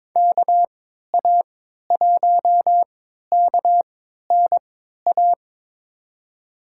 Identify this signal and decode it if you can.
Morse code, heard as a keyed tone, about 22 wpm, spaced as for 13 wpm, 700 Hz. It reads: KA1KNA
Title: USA Call Signs - 6 Characters - 3x Character Spacing - 22wpm